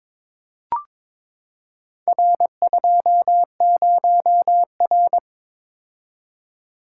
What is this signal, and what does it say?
Morse code, heard as a keyed tone, about 22 wpm, 700 Hz. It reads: R20R
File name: practice / 22wpm / Four Letter-Number Combinations - 22wpm